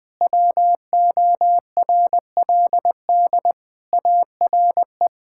Morse code, heard as a keyed tone, about 20 wpm, 700 Hz. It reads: WORLD ARE
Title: Sentences with 3-Word Phrases - 20wpm